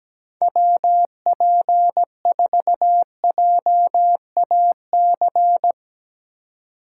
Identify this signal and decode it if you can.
Morse code, heard as a keyed tone, about 17 wpm, 700 Hz. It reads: WP4JAC
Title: USA Call Signs - 6 Characters - 4x Character Spacing - 17wpm